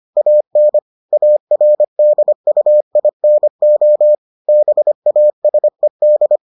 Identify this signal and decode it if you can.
Morse code, heard as a keyed tone, about 25 wpm, 600 Hz. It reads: AN ARDUINO BASED